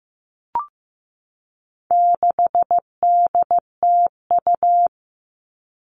Morse code, heard as a keyed tone, about 15 wpm, 700 Hz. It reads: 6DTU
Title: All Cut Numbers - 4-Digit 15wpm